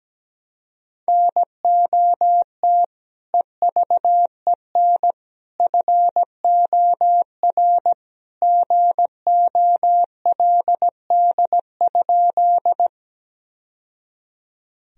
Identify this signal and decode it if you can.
Morse code, heard as a keyed tone, about 17 wpm, 700 Hz. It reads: NOT EVEN FOR GOLD?
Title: Sentences from Top 1000 Words - Review Part 3 - 17wpm